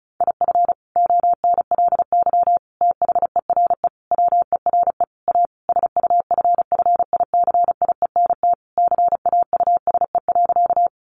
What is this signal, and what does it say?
Morse code, heard as a keyed tone, about 34 wpm, 700 Hz. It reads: IF ONLY THERE WERE A SUFFICIENT CAUSE.
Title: Sentences from Top 1400 Words 35wpm